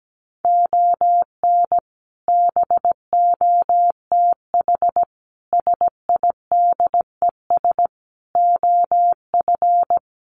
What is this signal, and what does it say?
Morse code, heard as a keyed tone, about 17 wpm, 700 Hz. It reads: ON BOTH SIDES OF
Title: Sets of 5 Words 17wpm